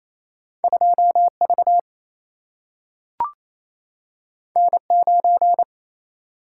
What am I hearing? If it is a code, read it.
Morse code, heard as a keyed tone, about 28 wpm, 700 Hz. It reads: 2V N9
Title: Single-Letter Single-Number 28wpm